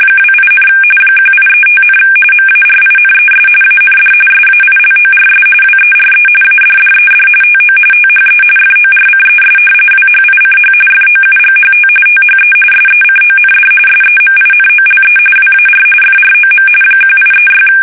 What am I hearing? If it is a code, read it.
RTTY, about 45.45 baud. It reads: 28, 1835, MARINES AND SAILORS WERE ADDED TO THE FORCES WHICH FOUGHT THE SECOND SEMINOLE WAR FROM 1836
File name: fsk850 / usnavy2